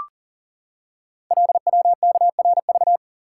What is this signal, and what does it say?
Morse code, heard as a keyed tone, about 42 wpm, 690 Hz. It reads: LWKRV
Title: Code Groups - 5 Letters - 40wpm